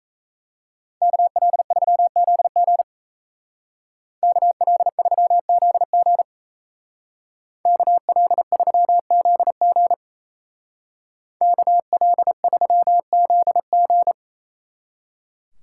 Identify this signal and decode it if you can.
Morse code, heard as a keyed tone, about 36 wpm, 700 Hz. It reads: KL3ZG KL3ZG KL3ZG KL3ZG